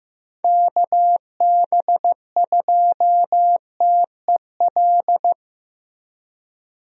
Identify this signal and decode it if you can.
Morse code, heard as a keyed tone, about 14 wpm, 700 Hz. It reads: KB2TEL
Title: USA Call Signs - 6 Characters - Repeated 3x - 15wpm